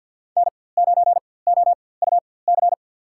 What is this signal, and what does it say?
Morse code, heard as a keyed tone, about 52 wpm, spaced as for 24 wpm, 700 Hz. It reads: N9OUC
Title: USA Call Signs - 4x Character Spacing - 50wpm